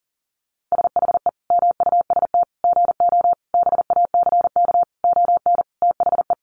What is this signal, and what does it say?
Morse code, heard as a keyed tone, about 40 wpm, 700 Hz. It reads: SHE MUST GO BACK ON THE
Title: Sentences from Top 1200 Words 40wpm